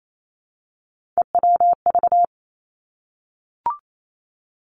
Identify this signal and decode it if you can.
Morse code, heard as a keyed tone, about 28 wpm, 700 Hz. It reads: EWV
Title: Three Challenging Letter-Number Combinations 28wpm